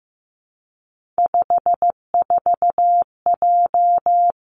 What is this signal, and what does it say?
Morse code, heard as a keyed tone, about 15 wpm, 700 Hz. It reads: 54J
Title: Three Challenging Letter-Number Combinations 15wpm